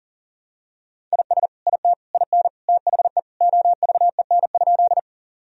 Morse code, heard as a keyed tone, about 40 wpm, 700 Hz. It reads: IS IT IN THE OVEN?
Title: Sentences from Top 1000 Words - Review Part 4 - 40wpm